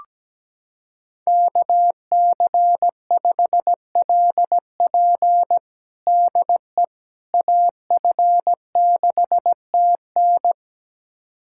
Morse code, heard as a keyed tone, about 17 wpm, 700 Hz. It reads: KC5LP DE AF6TN
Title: USA Callsign DE Callsign - 5 Characters - 17wpm